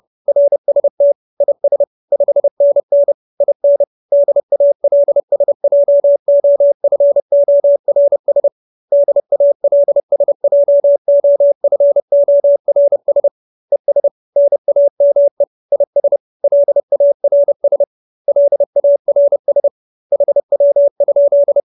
Morse code, heard as a keyed tone, about 30 wpm, 570 Hz. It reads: RST IS 5NN IN DALSJOFORS DALSJOFORS ES NAME IS LARS LARS HW?